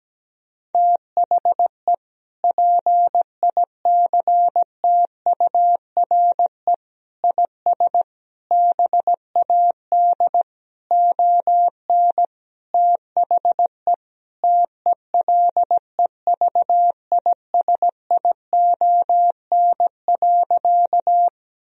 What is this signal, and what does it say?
Morse code, heard as a keyed tone, about 17 wpm, 700 Hz. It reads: THE PICTURE IS BAD ON THE TELEVISION.
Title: Sentences from Top 1000 Words - Review Part 4 - 17wpm